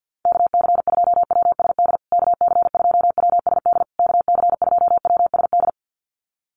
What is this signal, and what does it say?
Morse code, heard as a keyed tone, about 50 wpm, 700 Hz. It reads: KC2WSD KC2WSD KC2WSD
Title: USA Call Signs Repeated 3x 50wpm